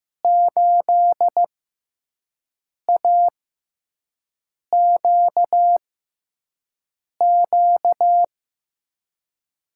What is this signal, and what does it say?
Morse code, heard as a keyed tone, about 15 wpm, spaced as for 5 wpm, 700 Hz. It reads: 8AQQ
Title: USA Call Signs - 6 Characters - 6x Character Spacing - 15wpm